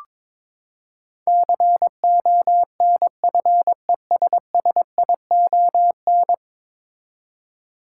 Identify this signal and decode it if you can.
Morse code, heard as a keyed tone, about 21 wpm, 700 Hz. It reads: CONFESSION
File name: practice / 22wpm / Ten Letter Words - 22wpm